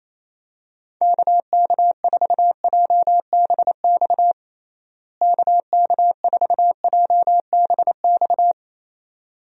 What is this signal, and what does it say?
Morse code, heard as a keyed tone, about 28 wpm, 700 Hz. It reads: KK4JBX KK4JBX
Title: USA Call Signs - 6 Characters - Repeated 2x - 28wpm